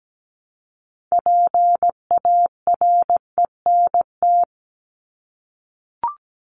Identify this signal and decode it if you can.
Morse code, heard as a keyed tone, about 17 wpm, 700 Hz. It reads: PARENT
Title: Top 1751-2000 Words 17wpm